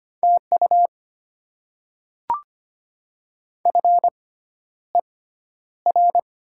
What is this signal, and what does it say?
Morse code, heard as a keyed tone, about 25 wpm, spaced as for 11 wpm, 700 Hz. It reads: TU FER
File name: Top 20 QSO Elements - 6x Character Spacing - 25wpm